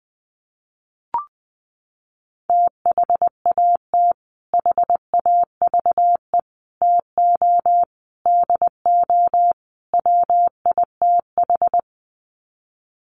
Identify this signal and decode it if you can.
Morse code, heard as a keyed tone, about 20 wpm, 700 Hz. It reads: THAT HAVE TO DO WITH
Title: Sets of 5 Words Encore2 20wpm